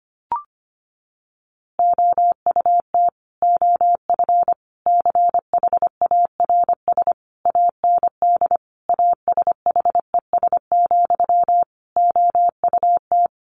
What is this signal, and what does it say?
Morse code, heard as a keyed tone, about 25 wpm, 700 Hz. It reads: OUT OF CHARS AND ASHES, OUT
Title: A Sound Of Thunder 25wpm